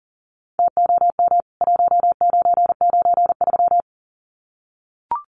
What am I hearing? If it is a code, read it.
Morse code, heard as a keyed tone, about 40 wpm, 700 Hz. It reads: TOM 1993
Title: CWT Exchange 40wpm